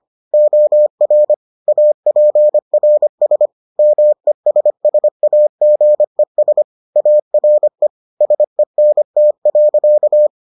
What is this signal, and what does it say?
Morse code, heard as a keyed tone, about 25 wpm, 600 Hz. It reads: OR APRS MESSAGES ARE SENT.